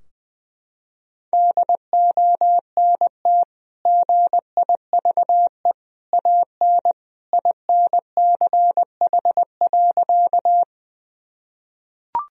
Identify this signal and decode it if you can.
Morse code, heard as a keyed tone, about 19 wpm, 700 Hz. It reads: DONT GIVE AN INCH.
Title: Sentences from Top 1000 Words - Review Part 3 - 20wpm